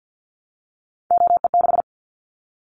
Morse code, heard as a keyed tone, about 50 wpm, 700 Hz. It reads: OE6
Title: Callsign Prefixes for Top 25 DXCC - 50wpm